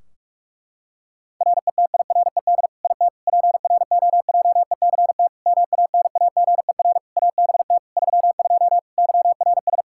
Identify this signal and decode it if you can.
Morse code, heard as a keyed tone, about 42 wpm, 700 Hz. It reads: RETIRED IT PROJECT MANAGER ABT 32 YRS
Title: Practice QSO - Exchange 3 - More Info - 45wpm